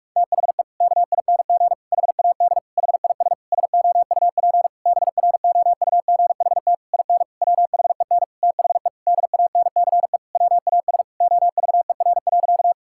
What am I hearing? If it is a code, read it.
Morse code, heard as a keyed tone, about 46 wpm, 700 Hz. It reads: THE KING HAD HIS SOUP BROUGHT IN WHEN THE DANCE WAS OVER.
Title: Sentences from Top 1000 Words - Review Part 6 - 45wpm